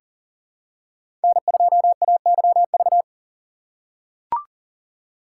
Morse code, heard as a keyed tone, about 40 wpm, 700 Hz. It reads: N2AYV
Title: USA Call Signs - 5 Characters - 4x Character Spacing - 40wpm